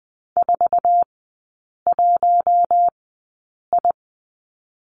Morse code, heard as a keyed tone, about 20 wpm, spaced as for 8 wpm, 700 Hz. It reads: 41I
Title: 047 Lesson 9e - Number 4 - Three Characters 20wpm